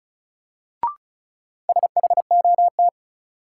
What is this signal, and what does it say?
Morse code, heard as a keyed tone, about 34 wpm, 700 Hz. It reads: SHOT
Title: Sentences from Top 1100 Words - New Word Drill - 35wpm